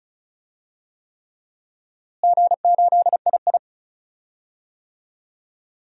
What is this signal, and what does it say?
Morse code, heard as a keyed tone, about 36 wpm, 700 Hz. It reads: G8II
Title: DXCC-100 Call Signs - 4 Characters - Repeated 3x - 35wpm